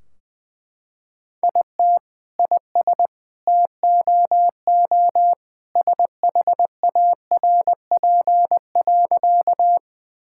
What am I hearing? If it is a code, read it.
Morse code, heard as a keyed tone, about 20 wpm, 700 Hz. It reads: IT IS TOO SHARP.
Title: Sentences from Top 1600 Words 20wpm